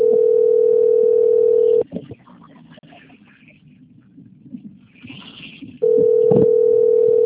Ringback tone (North America), 440 + 480 Hz.